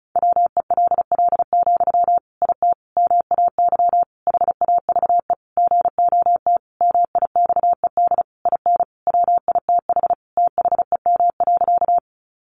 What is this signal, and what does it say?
Morse code, heard as a keyed tone, about 34 wpm, 700 Hz. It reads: WELL, IT MAY HAVE GOT MIXED IN WITH THEM.
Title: Sentences from Top 1500 Words 35wpm